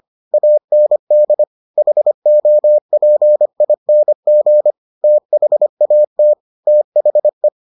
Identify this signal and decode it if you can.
Morse code, heard as a keyed tone, about 25 wpm, 600 Hz. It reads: AND HOPING THAT THE